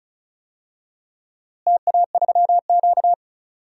Morse code, heard as a keyed tone, about 36 wpm, 700 Hz. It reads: TA3Q
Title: DXCC-100 Call Signs - 4 Characters - Repeated 3x - 35wpm